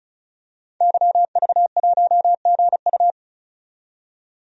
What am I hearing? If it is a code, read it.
Morse code, heard as a keyed tone, about 36 wpm, 700 Hz. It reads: YV1GU
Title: DXCC-100 Call Signs - 5 Characters - 35wpm